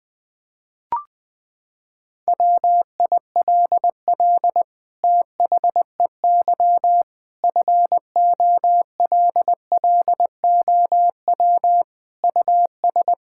Morse code, heard as a keyed tone, about 20 wpm, 700 Hz. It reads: WILL THEY FOLLOW US
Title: Sentences from Top 1600 Words 20wpm